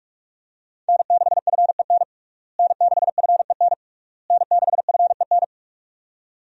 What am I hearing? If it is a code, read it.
Morse code, heard as a keyed tone, about 47 wpm, 700 Hz. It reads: N6FEN N6FEN N6FEN
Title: USA Call Signs - 5 Characters - Repeated 3x - 45wpm